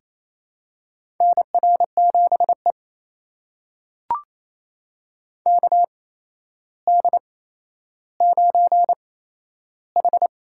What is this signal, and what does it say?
Morse code, heard as a keyed tone, about 28 wpm, spaced as for 12 wpm, 700 Hz. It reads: NR7E KD9H
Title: USA Call Signs - 4 Characters - 8x Character Spacing - 28wpm